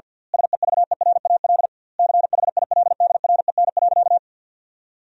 Morse code, heard as a keyed tone, about 50 wpm, 700 Hz. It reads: SEVERAL CHILDREN.
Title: 2x Word-Spacing - Sentences from Top 400 Words - 50wpm